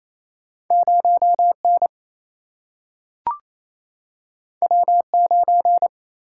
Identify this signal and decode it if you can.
Morse code, heard as a keyed tone, about 28 wpm, 700 Hz. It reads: 0N W9
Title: Single-Letter Single-Number 28wpm